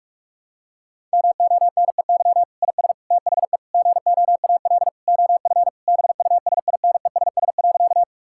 Morse code, heard as a keyed tone, about 47 wpm, 700 Hz. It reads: MONEY IS THE GOAL OF BUSINESS.
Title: Sentences from Top 1000 Words - Review Part 5 - 45wpm